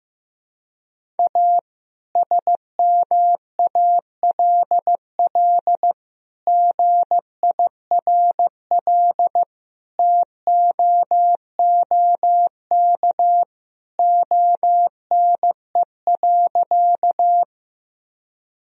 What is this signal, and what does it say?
Morse code, heard as a keyed tone, about 15 wpm, 700 Hz. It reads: A SMALL GIRL TOOK ONE.
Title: Sentences from Top 1000 Words - Review Part 1 - 15wpm